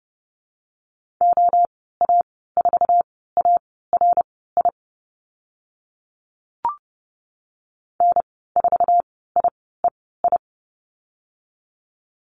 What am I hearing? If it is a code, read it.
Morse code, heard as a keyed tone, about 30 wpm, spaced as for 17 wpm, 700 Hz. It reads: OA4ARI N4IEI
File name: Lesson 14g - Number 2 - Call Signs 30wpm